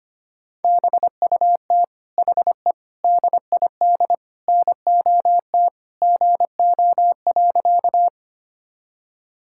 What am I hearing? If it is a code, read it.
Morse code, heard as a keyed tone, about 25 wpm, 700 Hz. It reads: BUT HE DID NOT GO.
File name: practice / 25wpm / Sentences with 4-Word Phrases - 25wpm